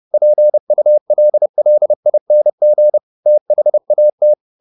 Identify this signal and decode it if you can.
Morse code, heard as a keyed tone, about 30 wpm, 600 Hz. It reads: PULLING THAT